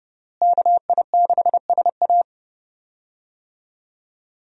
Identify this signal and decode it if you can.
Morse code, heard as a keyed tone, about 31 wpm, 700 Hz. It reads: KI6SA
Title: USA Call Signs - 5 Characters - Repeated 2x - 30wpm